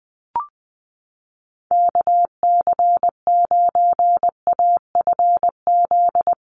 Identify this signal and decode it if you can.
Morse code, heard as a keyed tone, about 20 wpm, 700 Hz. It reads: KC9AFZ
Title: USA Call Signs Repeated 3x 20wpm